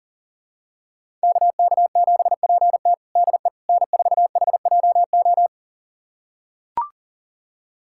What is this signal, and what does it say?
Morse code, heard as a keyed tone, about 40 wpm, 700 Hz. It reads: KK7PT DE N4HJO
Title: USA Callsign DE Callsign - 5 Characters - 40wpm